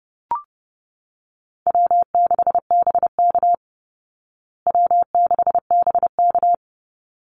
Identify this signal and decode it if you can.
Morse code, heard as a keyed tone, about 30 wpm, 700 Hz. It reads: W6BK W6BK
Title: Call Signs Repeated 3x - 30wpm